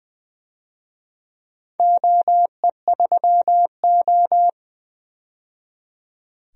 Morse code, heard as a keyed tone, about 20 wpm, 700 Hz. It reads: OE3O
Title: DXCC-100 Call Signs - 4 Characters - 20wpm